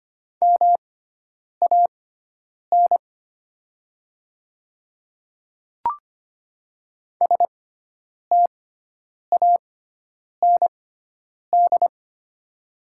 Morse code, heard as a keyed tone, about 25 wpm, spaced as for 9 wpm, 690 Hz. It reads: MAN STAND
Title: 6x Character Spacing - Top 300 Words - 25wpm